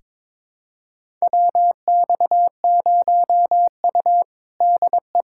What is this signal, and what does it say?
Morse code, heard as a keyed tone, about 21 wpm, 700 Hz. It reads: WX0U DE
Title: USA Callsign DE Callsign - 4 Characters - 22wpm